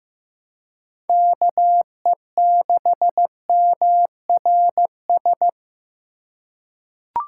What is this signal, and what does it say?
Morse code, heard as a keyed tone, about 15 wpm, 700 Hz. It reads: KE6MRS
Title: USA Call Signs - 6 Characters - 2x Character Spacing - 15wpm